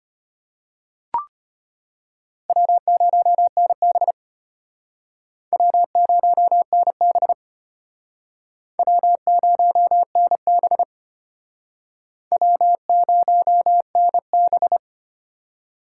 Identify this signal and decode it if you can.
Morse code, heard as a keyed tone, about 31 wpm, 700 Hz. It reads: W0NB W0NB W0NB W0NB